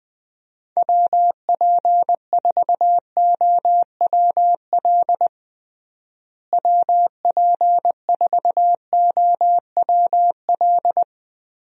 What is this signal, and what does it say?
Morse code, heard as a keyed tone, about 20 wpm, 700 Hz. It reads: WP4OWL WP4OWL
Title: USA Call Signs - 6 Characters - Repeated 3x - 20wpm